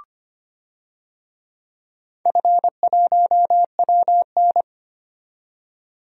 Morse code, heard as a keyed tone, about 25 wpm, 700 Hz. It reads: F1WN
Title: DXCC-100 Call Signs - 4 Characters - 25wpm